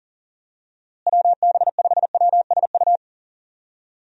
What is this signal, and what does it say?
Morse code, heard as a keyed tone, about 40 wpm, 700 Hz. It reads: WB5WSU
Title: USA Call Signs - 6 Characters - Repeated 2x - 40wpm